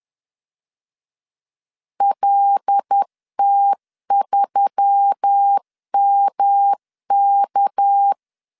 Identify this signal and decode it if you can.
Morse code, heard as a keyed tone, about 10 wpm, 800 Hz. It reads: LT3MK
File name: CW / DET / 1 6 1500C S C 10wpm 4252